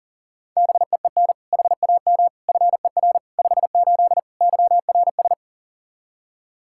Morse code, heard as a keyed tone, about 40 wpm, 700 Hz. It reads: BEEN HAM FER 58 YRS
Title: Practice QSO - Exchange 3 - More Info - 40wpm